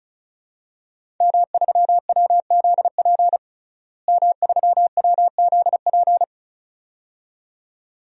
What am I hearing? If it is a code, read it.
Morse code, heard as a keyed tone, about 36 wpm, 700 Hz. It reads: M3WZP M3WZP
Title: DXCC-100 Call Signs - 5 Characters - Repeated 2x - 35wpm